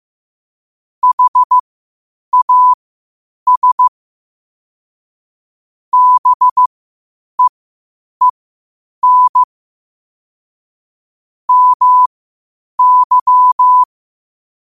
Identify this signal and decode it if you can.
Morse code, heard as a keyed tone, about 15 wpm, spaced as for 8 wpm, 1000 Hz. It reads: HAS BEEN MY
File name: qotdfiles / 8wpm